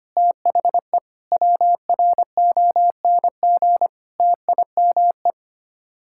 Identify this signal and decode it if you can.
Morse code, heard as a keyed tone, about 25 wpm, 700 Hz. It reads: THE WRONG TIME
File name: Sets of 9 Words - 25wpm